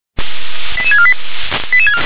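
Five-tone selective call (ZVEI), heard as quick runs of five tones, tones of about 70 ms. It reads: 88648
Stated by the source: ZVEI five-tone selective call received on an SDR.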